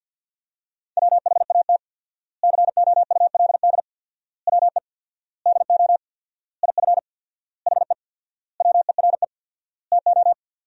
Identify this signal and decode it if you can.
Morse code, heard as a keyed tone, about 50 wpm, 700 Hz. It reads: WHAT COULD WE DO IF HE WERE TO
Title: 4x Word-Spacing - Sentences from Top 400 Words - 50wpm